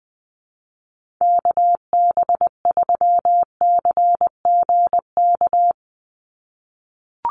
Morse code, heard as a keyed tone, about 20 wpm, 700 Hz. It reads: KB3CGK
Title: USA Call Signs 20wpm